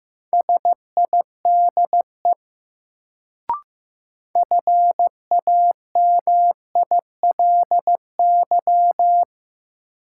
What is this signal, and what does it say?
Morse code, heard as a keyed tone, about 15 wpm, 700 Hz. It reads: SIDE FAMILY
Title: Sentences from Top 400 Words - New Word Drill - 15wpm